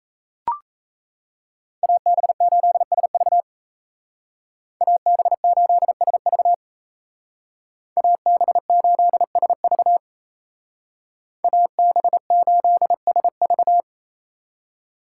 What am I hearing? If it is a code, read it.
Morse code, heard as a keyed tone, about 36 wpm, 700 Hz. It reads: AB8SV AB8SV AB8SV AB8SV